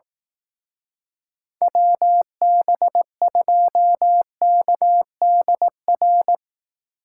Morse code, heard as a keyed tone, about 18 wpm, 700 Hz. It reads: WB2KDR
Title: Speed Racing - USA 2x3 Call Signs 15wpm